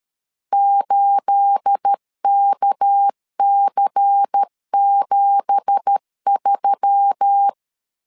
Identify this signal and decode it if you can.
Morse code, heard as a keyed tone, about 13 wpm, 800 Hz. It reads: 8KC73
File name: LEZIONI / 10 SIGNS / 10A SLASH 12wpm 2314